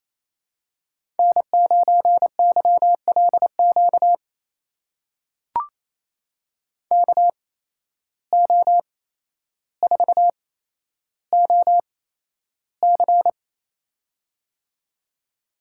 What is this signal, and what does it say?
Morse code, heard as a keyed tone, about 28 wpm, spaced as for 12 wpm, 700 Hz. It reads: N9YLQ KO4OC
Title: USA Call Signs - 5 Characters - 8x Character Spacing - 28wpm